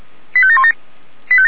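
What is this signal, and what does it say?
Five-tone selective call (ZVEI), heard as quick runs of five tones, tones of about 70 ms. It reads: 85717
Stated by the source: ZVEI five-tone selective call received by SDR, from the Feuerwehr Orlamünde (fire brigade) audio archive.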